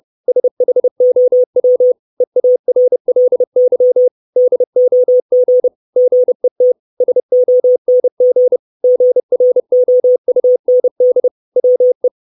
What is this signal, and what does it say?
Morse code, heard as a keyed tone, about 30 wpm, 500 Hz. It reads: SHOW EARLY DOG GET SONG GROUND WE
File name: 632 Words 30WPM